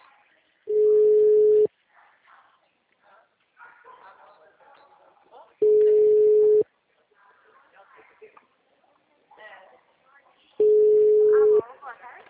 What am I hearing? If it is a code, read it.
Ringback tone (Europe, 425 Hz), with speech.